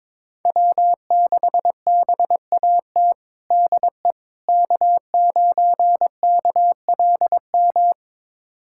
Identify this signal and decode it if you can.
Morse code, heard as a keyed tone, about 22 wpm, 700 Hz. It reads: W6BAT DE K9KLM
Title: USA Callsign DE Callsign - 5 Characters - 22wpm